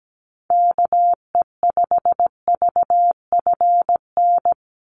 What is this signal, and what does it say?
Morse code, heard as a keyed tone, about 17 wpm, 700 Hz. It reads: KE5VFN
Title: USA Call Signs 17wpm